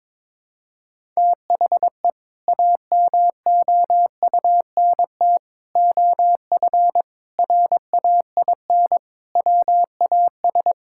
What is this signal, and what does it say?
Morse code, heard as a keyed tone, about 21 wpm, 700 Hz. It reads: THE AMOUNT OF RAIN WAS
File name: practice / 22wpm / Sentences with 3-Word Phrases - 22wpm